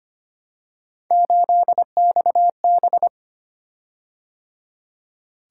Morse code, heard as a keyed tone, about 25 wpm, 700 Hz. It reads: 8XB
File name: Three Letter-Number Combinations - 25wpm